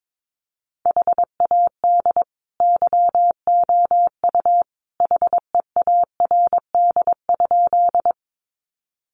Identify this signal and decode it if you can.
Morse code, heard as a keyed tone, about 22 wpm, 700 Hz. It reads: HAD YOU HEARD?
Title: Sentences from Top 400 Words 22wpm